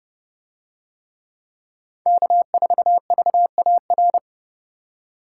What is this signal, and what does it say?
Morse code, heard as a keyed tone, about 30 wpm, 700 Hz. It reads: K4VAR